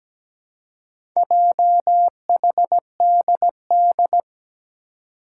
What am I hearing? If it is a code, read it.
Morse code, heard as a keyed tone, about 17 wpm, 700 Hz. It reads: JHDD
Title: Four Challenging Letter-Number Combinations 17wpm